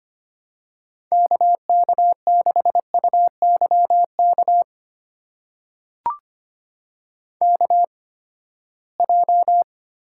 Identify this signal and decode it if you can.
Morse code, heard as a keyed tone, about 25 wpm, spaced as for 17 wpm, 700 Hz. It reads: KK6UYK KJ
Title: USA Call Signs - 6 Characters - 8x Character Spacing - 25wpm